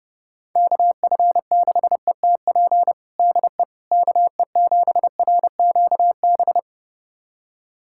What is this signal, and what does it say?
Morse code, heard as a keyed tone, about 30 wpm, 700 Hz. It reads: KF6ETP DE KE7RQB
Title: USA Callsign DE Callsign - 6 Characters - 30wpm